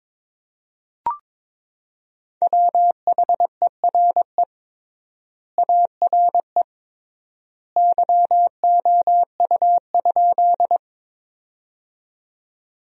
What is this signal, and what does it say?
Morse code, heard as a keyed tone, about 22 wpm, 700 Hz. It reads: WHERE ARE YOU?
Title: 3x Word-Spacing - Sentences from Top 200 Words - 22wpm